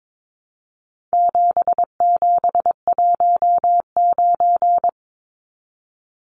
Morse code, heard as a keyed tone, about 22 wpm, 700 Hz. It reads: 7719
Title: Numbers - 4-Digit 22wpm